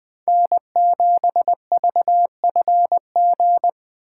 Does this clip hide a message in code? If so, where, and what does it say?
Yes: Morse code from 0.3 to 3.7 s, heard as a keyed tone, about 20 wpm, 700 Hz. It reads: N7VFG